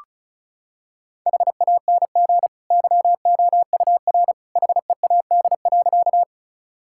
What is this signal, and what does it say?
Morse code, heard as a keyed tone, about 34 wpm, 700 Hz. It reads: HANG YOUR HEAD.
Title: Sentences from Top 1600 Words 35wpm